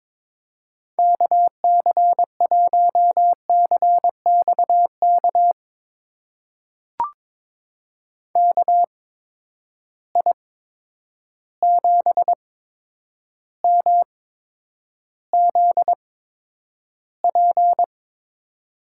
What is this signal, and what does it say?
Morse code, heard as a keyed tone, about 22 wpm, spaced as for 9 wpm, 700 Hz. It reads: KC1CXK KI7MZP
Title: USA Call Signs - 6 Characters - 8x Character Spacing - 22wpm